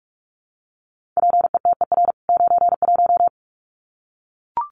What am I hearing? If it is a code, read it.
Morse code, heard as a keyed tone, about 44 wpm, 700 Hz. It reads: PETER 91
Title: CWT Exchange 45wpm